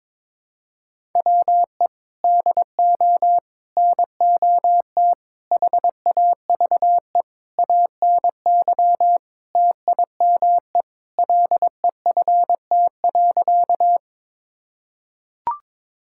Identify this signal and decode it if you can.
Morse code, heard as a keyed tone, about 21 wpm, 700 Hz. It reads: WE DO NOT HAVE ANY TIME LEFT.
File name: practice / 22wpm / Sentences with 4-Word Phrases - 22wpm